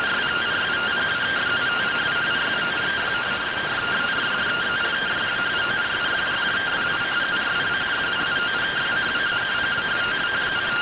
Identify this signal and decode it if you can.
Two-tone teleprinter signal, about 170 Hz shift.